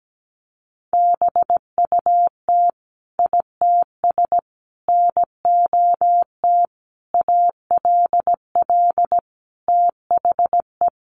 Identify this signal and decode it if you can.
Morse code, heard as a keyed tone, about 17 wpm, 700 Hz. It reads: BUT ITS NOT ALL THE
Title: Sentences from Top 600 Words 17wpm